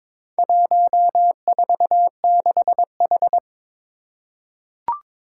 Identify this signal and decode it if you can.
Morse code, heard as a keyed tone, about 23 wpm, 700 Hz. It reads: 146H
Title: Four Challenging Letter-Number Combinations 22wpm